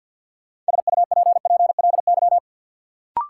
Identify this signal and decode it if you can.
Morse code, heard as a keyed tone, about 50 wpm, 700 Hz. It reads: SUPPLY
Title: Sentences from Top 1000 Words - Review Part 4 - New Word Drill - 50wpm